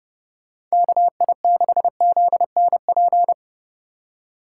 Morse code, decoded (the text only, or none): KI6ZNP